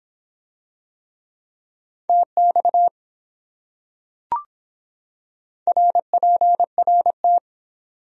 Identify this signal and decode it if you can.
Morse code, heard as a keyed tone, about 24 wpm, 700 Hz. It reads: TX RPRT